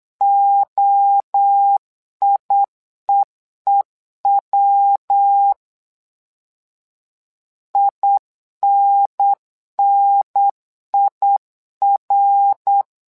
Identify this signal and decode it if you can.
Morse code, heard as a keyed tone, about 8 wpm, 800 Hz. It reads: OIEEW INNIR
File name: LEZIONI / 3 MWRO / 3 MWRO 8wpm 2606